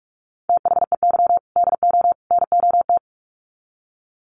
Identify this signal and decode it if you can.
Morse code, heard as a keyed tone, about 44 wpm, 700 Hz. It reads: THEY DO NOT
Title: Sets of 3 Words 45wpm